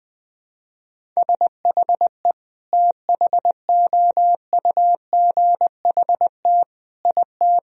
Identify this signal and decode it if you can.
Morse code, heard as a keyed tone, about 20 wpm, 700 Hz. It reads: SHE THOUGHT IT
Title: Sentences from Top 1000 Words - Review Part 1 - 20wpm